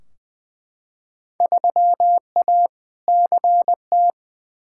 Morse code, heard as a keyed tone, about 21 wpm, 700 Hz. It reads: 3A CT
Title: Field Day Exchanges - 20wpm